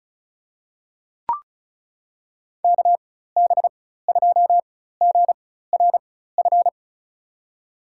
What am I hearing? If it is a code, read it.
Morse code, heard as a keyed tone, about 36 wpm, spaced as for 16 wpm, 700 Hz. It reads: KB2GRF
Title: USA Call Signs - 6 Characters - 4x Character Spacing - 35wpm